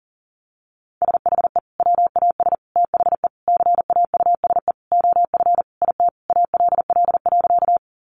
Morse code, heard as a keyed tone, about 40 wpm, 700 Hz. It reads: SHE WAS THE CAUSE OF IT ALL.
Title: Sentences from Top 200 Words 40wpm